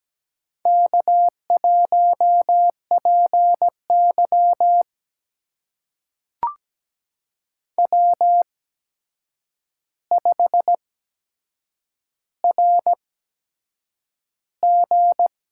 Morse code, heard as a keyed tone, about 17 wpm, spaced as for 7 wpm, 700 Hz. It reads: K1PY W5RG
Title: USA Call Signs - 4 Characters - 8x Character Spacing - 17wpm